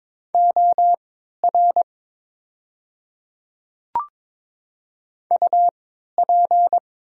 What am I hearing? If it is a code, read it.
Morse code, heard as a keyed tone, about 22 wpm, spaced as for 12 wpm, 700 Hz. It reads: OR UP
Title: 3x Character Spacing - Top 100 Words - 22wpm